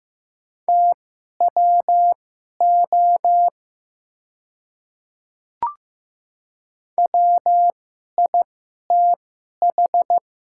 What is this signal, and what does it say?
Morse code, heard as a keyed tone, about 15 wpm, spaced as for 11 wpm, 700 Hz. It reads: TWO WITH
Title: 2x Character Spacing - Top 100 Words - 15wpm